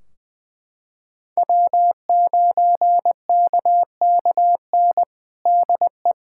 Morse code, heard as a keyed tone, about 20 wpm, 700 Hz. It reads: W9KKN DE
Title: Practice QSO - Exchange 4 - Ending - 20wpm